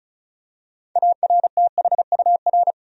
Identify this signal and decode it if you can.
Morse code, heard as a keyed tone, about 34 wpm, 700 Hz. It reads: ARTHUR
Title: USA Common Names - 35wpm